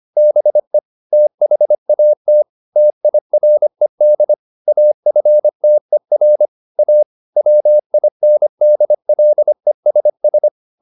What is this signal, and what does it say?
Morse code, heard as a keyed tone, about 25 wpm, 600 Hz. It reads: BE THAT TIRED AFTER A WINDLESS
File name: the old man and the sea part 1 25wpm